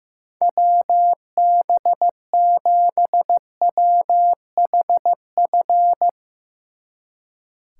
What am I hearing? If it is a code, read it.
Morse code, heard as a keyed tone, about 16 wpm, 700 Hz. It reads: WB7WHF
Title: Speed Racing - USA 2x3 Call Signs 15wpm